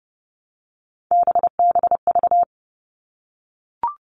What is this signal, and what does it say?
Morse code, heard as a keyed tone, about 30 wpm, 700 Hz. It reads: BBV
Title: All Cut Numbers - 3-Digit 30wpm